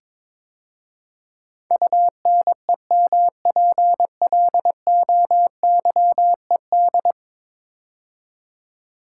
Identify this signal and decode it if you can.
Morse code, heard as a keyed tone, about 22 wpm, 700 Hz. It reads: UNEMPLOYED
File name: practice / 22wpm / 10 Letter Words - First 3 Letters Given - 22wpm